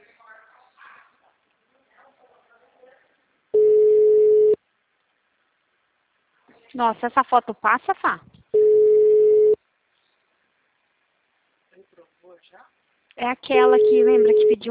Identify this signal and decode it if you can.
Ringback tone (Europe, 425 Hz), with speech.
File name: out- -2004- - - .44356